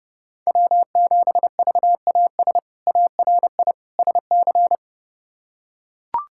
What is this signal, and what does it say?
Morse code, heard as a keyed tone, about 30 wpm, 700 Hz. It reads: W7VAS ARI SC
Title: K1USN SST QSO Practice - 30wpm